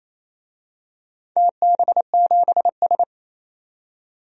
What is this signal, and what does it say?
Morse code, heard as a keyed tone, about 29 wpm, 700 Hz. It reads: TB7S
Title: DXCC-100 Call Signs - 4 Characters - 28wpm